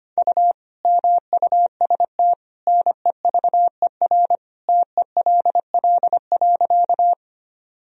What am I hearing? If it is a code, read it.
Morse code, heard as a keyed tone, about 25 wpm, 700 Hz. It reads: U MUST NEVER TELL.
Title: Sentences from Top 300 Words and Abbreviations 25wpm